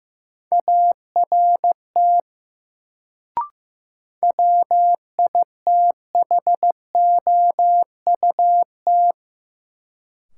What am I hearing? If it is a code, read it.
Morse code, heard as a keyed tone, about 15 wpm, 700 Hz. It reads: ART WITHOUT
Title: Sentences from Top 600 Words - New Word Drill - 15wpm